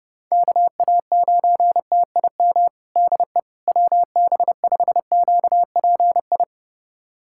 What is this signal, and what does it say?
Morse code, heard as a keyed tone, about 32 wpm, 700 Hz. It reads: KA9TIM DE WB5QPI
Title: USA Callsign DE Callsign - 6 Characters - 30wpm